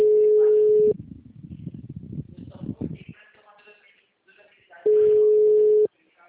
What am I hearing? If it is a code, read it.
Ringback tone (Europe, 425 Hz), with speech.